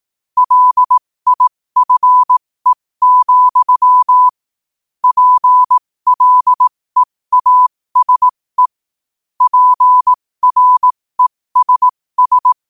Morse code, heard as a keyed tone, about 18 wpm, spaced as for 15 wpm, 1000 Hz. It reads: LIFE, PLEASE PRESS
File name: qotdfiles / 15wpm